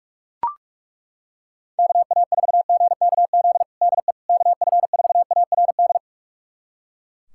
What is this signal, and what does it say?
Morse code, heard as a keyed tone, about 42 wpm, 690 Hz. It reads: KA4GKZ DE KF4ARD
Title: USA Callsign DE Callsign - 6 Characters - 45wpm